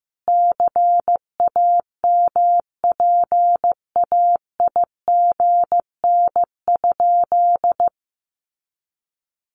Morse code, heard as a keyed tone, about 15 wpm, 700 Hz. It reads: CAMPAIGN?
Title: Sentences from Top 1200 Words 15wpm